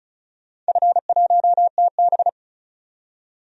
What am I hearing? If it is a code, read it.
Morse code, heard as a keyed tone, about 34 wpm, 700 Hz. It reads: F1TB
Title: DXCC-100 Call Signs - 4 Characters - 35wpm